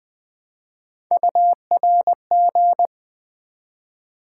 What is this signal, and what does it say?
Morse code, heard as a keyed tone, about 20 wpm, 700 Hz. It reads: URG
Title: Build A Word - 1 Letter At A Time - 3 to 5-Letter Words - 20wpm